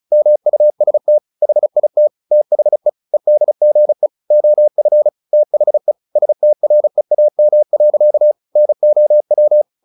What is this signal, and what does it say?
Morse code, heard as a keyed tone, about 34 wpm, 600 Hz. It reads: MUST HIT THE EDGE OF THE STREAM. NOW
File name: the old man and the sea 2 35wpm